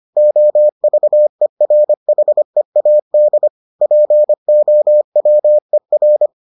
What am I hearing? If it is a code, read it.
Morse code, heard as a keyed tone, about 25 wpm, 600 Hz. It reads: OVERHEAD POWER